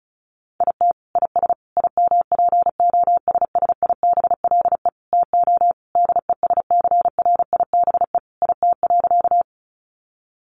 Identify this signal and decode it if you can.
Morse code, heard as a keyed tone, about 34 wpm, 700 Hz. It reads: IT IS IMPOSSIBLE TO DESCRIBE IT.
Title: Sentences from Top 1100 Words 35wpm